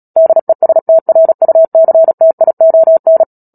Morse code, heard as a keyed tone, about 37 wpm, 650 Hz. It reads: DESTRUCTION
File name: Book Two / 10-Chapter 10 - Epilogue - 36 WPM